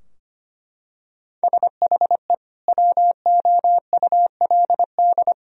Morse code, heard as a keyed tone, about 25 wpm, 700 Hz. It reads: SHE WOULD
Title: Sentences from Top 1000 Words - Review Part 2 - 25wpm